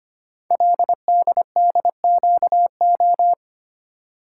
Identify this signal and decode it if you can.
Morse code, heard as a keyed tone, about 25 wpm, 700 Hz. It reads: LDDQO
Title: Code Groups - 5 Letters - 25wpm